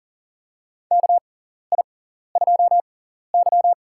Morse code, heard as a keyed tone, about 40 wpm, spaced as for 14 wpm, 700 Hz. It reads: KI2Y
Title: USA Call Signs - 4 Characters - 6x Character Spacing - 40wpm